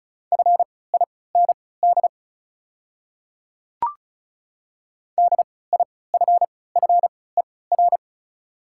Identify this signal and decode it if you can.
Morse code, heard as a keyed tone, about 36 wpm, spaced as for 20 wpm, 700 Hz. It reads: FIND DIFFER
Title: 3x Character Spacing - Top 200 Words - 35wpm